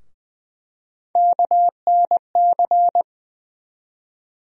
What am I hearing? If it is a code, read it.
Morse code, heard as a keyed tone, about 20 wpm, 700 Hz. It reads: KNC
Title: USA Call Sign Postfixes - 3 Letters 20wpm